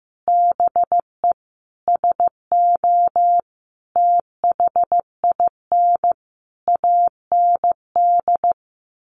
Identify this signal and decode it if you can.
Morse code, heard as a keyed tone, about 15 wpm, 700 Hz. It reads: BE SO THIN AND
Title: Sentences from Top 1200 Words 15wpm